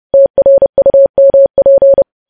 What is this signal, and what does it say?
Morse code, heard as a keyed tone, about 30 wpm, 550 Hz. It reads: TRUMP
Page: news headlines in morse code at 30 wpm